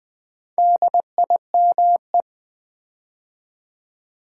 Morse code, heard as a keyed tone, about 20 wpm, 700 Hz. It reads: DIME
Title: Build A Word - 2 Letters At A Time - 4-Letter Words - 20wpm